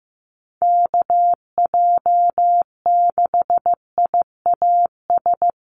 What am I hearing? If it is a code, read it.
Morse code, heard as a keyed tone, about 15 wpm, 700 Hz. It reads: KJ6IAS
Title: USA Call Signs Repeated 2x 15wpm